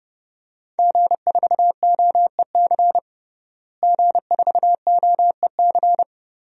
Morse code, heard as a keyed tone, about 30 wpm, 700 Hz. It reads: G4OEC G4OEC
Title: DXCC-100 Call Signs - 5 Characters - Repeated 2x - 30wpm